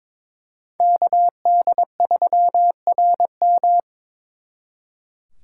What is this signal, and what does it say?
Morse code, heard as a keyed tone, about 26 wpm, 700 Hz. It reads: KD3RM
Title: Speed Racing - USA 2x2 Call Signs 22wpm